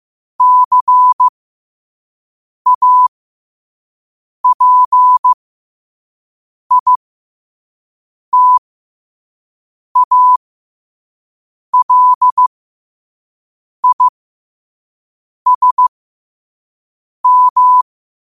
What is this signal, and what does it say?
Morse code, heard as a keyed tone, about 15 wpm, spaced as for 5 wpm, 1000 Hz. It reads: CAPITALISM
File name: qotdfiles / 5wpm